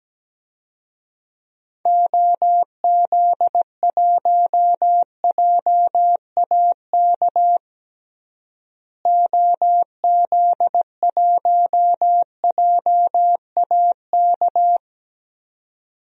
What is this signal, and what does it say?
Morse code, heard as a keyed tone, about 17 wpm, 700 Hz. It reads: OZ1JAK OZ1JAK
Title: DXCC-100 Call Signs - 6 Characters - Repeated 2x - 17wpm